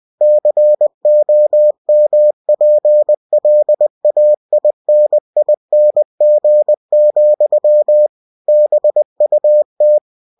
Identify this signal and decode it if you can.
Morse code, heard as a keyed tone, about 20 wpm, 600 Hz. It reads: COMPLAINING, BUT
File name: cw-books / winnie-the-pooh / winnie-20wpm-page-0045